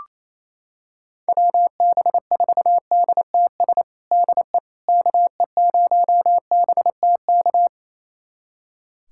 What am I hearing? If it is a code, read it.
Morse code, heard as a keyed tone, about 28 wpm, 700 Hz. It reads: WB4DTS DE KE0BTK
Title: USA Callsign DE Callsign - 6 Characters - 28wpm